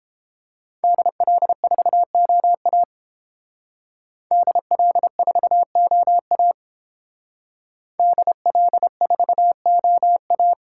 Morse code, heard as a keyed tone, about 29 wpm, 700 Hz. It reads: DL4OA DL4OA DL4OA